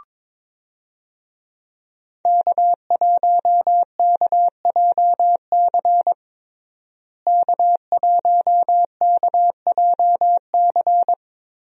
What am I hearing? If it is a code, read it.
Morse code, heard as a keyed tone, about 21 wpm, 700 Hz. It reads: K1KJC K1KJC
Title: USA Call Signs - 5 Characters - Repeated 3x - 22wpm